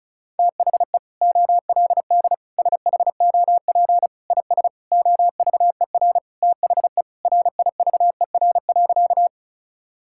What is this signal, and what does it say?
Morse code, heard as a keyed tone, about 36 wpm, 690 Hz. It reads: THE OLD SHOP IS OVER THE RIVER.
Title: Sentences from Top 1600 Words 35wpm